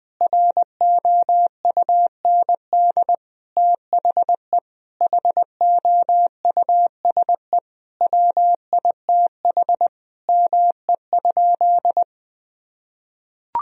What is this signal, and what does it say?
Morse code, heard as a keyed tone, about 21 wpm, 700 Hz. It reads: ROUND THE HOUSE WITH ME?
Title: Sentences from Top 1000 Words - Review Part 1 - 20wpm